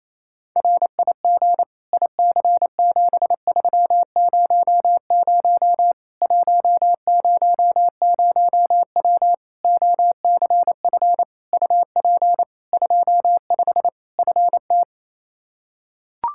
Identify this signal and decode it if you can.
Morse code, heard as a keyed tone, about 28 wpm, 700 Hz. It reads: RIG IC7300 100W OCF UP 25 FT
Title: Practice QSO - Exchange 2 - Info - 28wpm